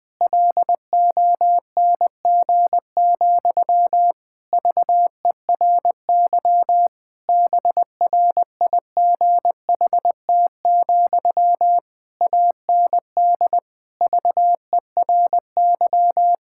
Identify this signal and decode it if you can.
Morse code, heard as a keyed tone, about 21 wpm, 700 Hz. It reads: LONG, VERY BRIGHT, AND VERY
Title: Sentences from Top 1600 Words 20wpm